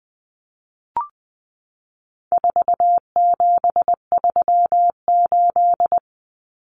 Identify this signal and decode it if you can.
Morse code, heard as a keyed tone, about 20 wpm, 700 Hz. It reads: 4738
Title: Numbers - 4-Digit 20wpm